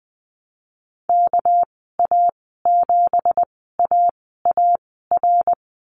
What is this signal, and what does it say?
Morse code, heard as a keyed tone, about 20 wpm, spaced as for 15 wpm, 700 Hz. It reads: KA7AAR
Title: 207 Lesson 33g - Letter j - Call Signs 20wpm